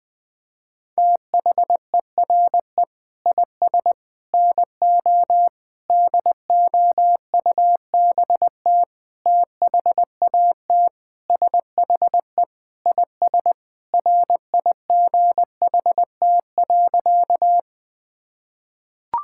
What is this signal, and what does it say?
Morse code, heard as a keyed tone, about 20 wpm, 700 Hz. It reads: THERE IS NO DOUBT THAT SHE IS RIGHT.
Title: Sentences with 5-Word Phrases - 20wpm